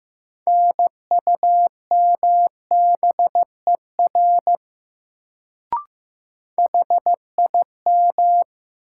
Morse code, heard as a keyed tone, about 15 wpm, 700 Hz. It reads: NUMBER HIM
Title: Sentences from Top 100 Words - New Word Drill - 15wpm